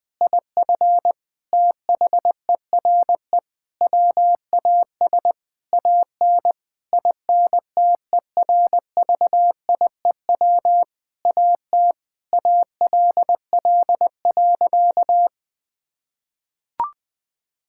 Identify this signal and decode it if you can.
Morse code, heard as a keyed tone, about 20 wpm, 700 Hz. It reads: IF THERE WAS AN INTERVIEW AT ALL.
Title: Sentences from Top 1600 Words 20wpm